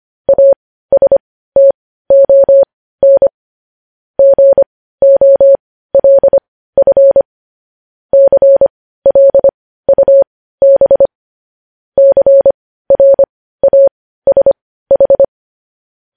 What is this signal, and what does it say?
Morse code, heard as a keyed tone, about 25 wpm, spaced as for 15 wpm, 550 Hz. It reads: ASTON GOLF CLUB CRASH
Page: News Headlines in Morse Code at 15 WPM